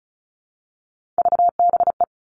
Morse code, heard as a keyed tone, about 36 wpm, 700 Hz. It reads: VBE
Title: Three Challenging Letter-Number Combinations 35wpm